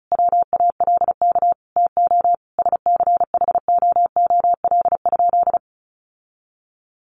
Morse code, heard as a keyed tone, about 34 wpm, 700 Hz. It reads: WALK TO SCHOOL?
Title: Sentences from Top 1400 Words 35wpm